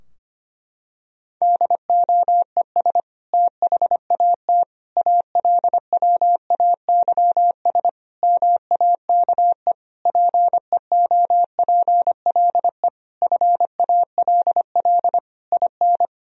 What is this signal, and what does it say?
Morse code, heard as a keyed tone, about 25 wpm, 700 Hz. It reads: DOES THAT ALWAYS MAKE PEOPLE FALL IN
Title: Sentences from Top 1000 Words - Review Part 2 - 25wpm